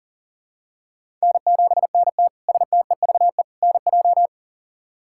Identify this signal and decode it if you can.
Morse code, heard as a keyed tone, about 40 wpm, 700 Hz. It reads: N7NT STEVE NJ
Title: K1USN SST QSO Practice - 40wpm